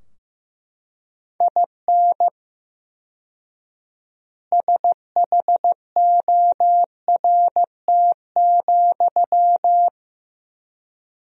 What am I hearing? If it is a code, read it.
Morse code, heard as a keyed tone, about 16 wpm, 700 Hz. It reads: IN SHORT,
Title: 4x Word-Spacing - Sentences from Top 400 Words - 15wpm